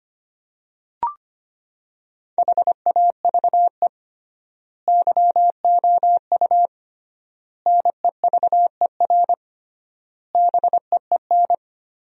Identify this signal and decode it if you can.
Morse code, heard as a keyed tone, about 25 wpm, 700 Hz. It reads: HAVE YOU NEVER BEEN
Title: 3x Word-Spacing - Sentences from Top 300 Words - 25wpm